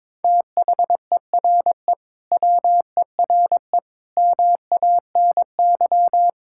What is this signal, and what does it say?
Morse code, heard as a keyed tone, about 22 wpm, 700 Hz. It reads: THERE WERE MANY
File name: practice / 22wpm / Sentences with 3-Word Phrases - 22wpm